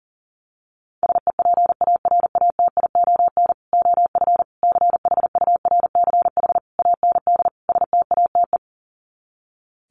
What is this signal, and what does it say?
Morse code, heard as a keyed tone, about 40 wpm, 700 Hz. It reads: SEPARATION OF CHURCH AND STATE
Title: Sets of 5 Words 40wpm